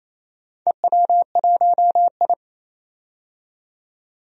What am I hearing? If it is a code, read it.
Morse code, heard as a keyed tone, about 27 wpm, 700 Hz. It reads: EW1I
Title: DXCC-100 Call Signs - 4 Characters - 28wpm